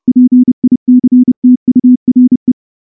Morse code, heard as a keyed tone, about 30 wpm, 250 Hz. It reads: PICTURE
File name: morse / morse-training-30WPM